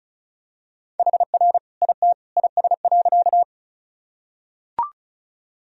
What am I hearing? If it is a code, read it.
Morse code, heard as a keyed tone, about 35 wpm, 700 Hz. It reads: HR IT IS.
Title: Sentences from Top 200 Words and Abbreviations 35wpm